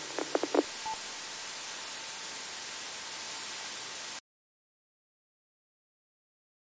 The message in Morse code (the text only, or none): none